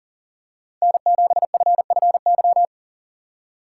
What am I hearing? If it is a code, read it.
Morse code, heard as a keyed tone, about 38 wpm, 700 Hz. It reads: N7FFY
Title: USA Call Signs - 2x Character Spacing - 40wpm